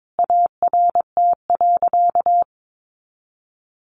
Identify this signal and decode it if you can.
Morse code, heard as a keyed tone, about 22 wpm, 700 Hz. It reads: ART.
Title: Sentences from Top 1200 Words 22wpm